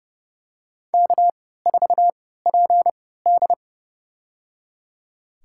Morse code, heard as a keyed tone, about 30 wpm, spaced as for 17 wpm, 700 Hz. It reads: K4PD
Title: USA Call Signs - 4 Characters - 3x Character Spacing - 30wpm